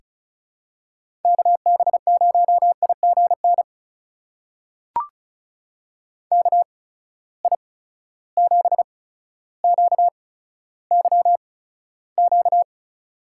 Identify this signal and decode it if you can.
Morse code, heard as a keyed tone, about 36 wpm, spaced as for 15 wpm, 700 Hz. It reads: KB0IGN KI7QYQ
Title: USA Call Signs - 6 Characters - 8x Character Spacing - 35wpm